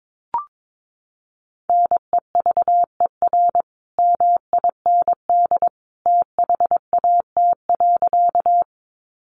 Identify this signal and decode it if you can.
Morse code, heard as a keyed tone, about 22 wpm, 700 Hz. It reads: NEVER MIND THAT.
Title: Sentences from Top 500 Words 22wpm